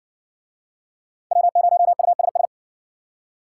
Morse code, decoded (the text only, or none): W0USS